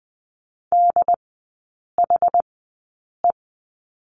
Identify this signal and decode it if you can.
Morse code, heard as a keyed tone, about 20 wpm, spaced as for 8 wpm, 700 Hz. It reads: DHE
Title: 081 Lesson 14e - Number 2 - Three Characters 20wpm